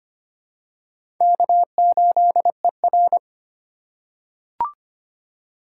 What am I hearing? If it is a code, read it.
Morse code, heard as a keyed tone, about 25 wpm, 700 Hz. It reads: K8ER
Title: USA Call Signs - 4 Characters - 2x Character Spacing - 25wpm